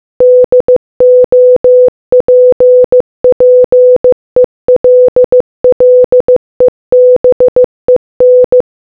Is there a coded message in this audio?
Yes: Morse code from 0.2 to 8.6 s, heard as a keyed tone, about 15 wpm, 500 Hz. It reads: DOPPELLEBEN